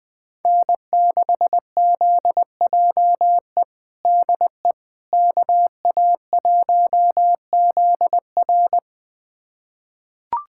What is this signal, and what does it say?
Morse code, heard as a keyed tone, about 20 wpm, 700 Hz. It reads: N6ZJE DE KA1ZR
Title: USA Callsign DE Callsign - 5 Characters - 20wpm